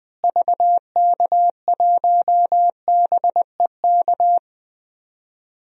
Morse code, heard as a keyed tone, about 20 wpm, 700 Hz. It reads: VK1BEK
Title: DXCC-100 Call Signs - 6 Characters - Repeated 3x - 20wpm